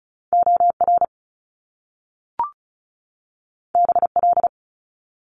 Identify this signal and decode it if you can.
Morse code, heard as a keyed tone, about 36 wpm, 700 Hz. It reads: OR BL
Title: Top Two Letter Combinations in Words 35wpm